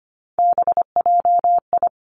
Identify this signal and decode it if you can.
Morse code, heard as a keyed tone, about 25 wpm, 700 Hz. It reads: BJI
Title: Three Challenging Letter-Number Combinations 25wpm